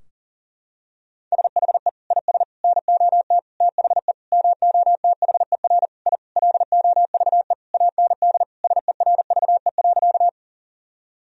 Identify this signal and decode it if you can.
Morse code, heard as a keyed tone, about 40 wpm, 700 Hz. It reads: SHE IS NOT THE MOTHER I LOVE AND SERVE.
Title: Sentences from Top 1000 Words - Review Part 2 - 40wpm